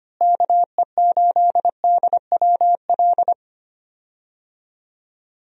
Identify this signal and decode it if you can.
Morse code, heard as a keyed tone, about 25 wpm, 700 Hz. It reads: KE8DWL
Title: USA Call Signs - 6 Characters - Repeated 3x - 25wpm